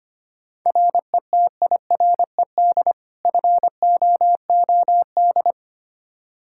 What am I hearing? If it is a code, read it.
Morse code, heard as a keyed tone, about 25 wpm, 700 Hz. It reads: RETIRED FOOD
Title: Practice QSO - Exchange 3 - More Info - 25wpm